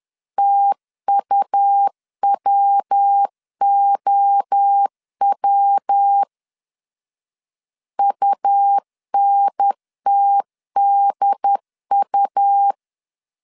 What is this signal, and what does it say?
Morse code, heard as a keyed tone, about 10 wpm, 800 Hz. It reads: TUWOW UNTDU
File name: CW / DET / 1 3 1500C S C 10wpm 3335